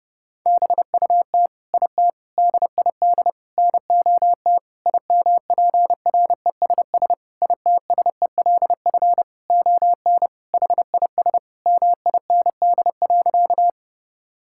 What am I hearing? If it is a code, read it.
Morse code, heard as a keyed tone, about 30 wpm, 700 Hz. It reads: BUT IT DID NOT IMPRESS ITSELF ON HIS MIND.
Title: Sentences from Top 1600 Words 30wpm